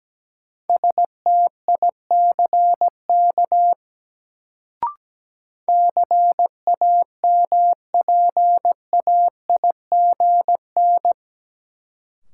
Morse code, heard as a keyed tone, about 17 wpm, 700 Hz. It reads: STICK CAMPAIGN
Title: Sentences from Top 1200 Words - New Word Drill Alt - 17wpm